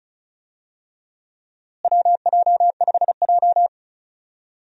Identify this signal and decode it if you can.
Morse code, heard as a keyed tone, about 34 wpm, 700 Hz. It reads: WJ5J
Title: USA Call Signs - 4 Characters - Repeated 2x - 35wpm